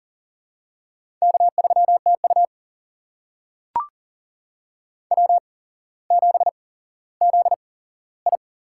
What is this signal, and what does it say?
Morse code, heard as a keyed tone, about 40 wpm, spaced as for 17 wpm, 700 Hz. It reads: K3TU W7ZI
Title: USA Call Signs - 4 Characters - 8x Character Spacing - 40wpm